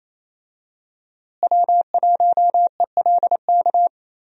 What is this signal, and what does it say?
Morse code, heard as a keyed tone, about 28 wpm, 700 Hz. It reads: W1ELK
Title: USA Call Signs - 5 Characters - Repeated 3x - 28wpm